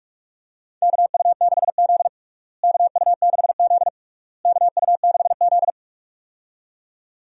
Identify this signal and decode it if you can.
Morse code, heard as a keyed tone, about 46 wpm, 700 Hz. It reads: KU6Z KU6Z KU6Z
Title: USA Call Signs - 4 Characters - Repeated 3x - 45wpm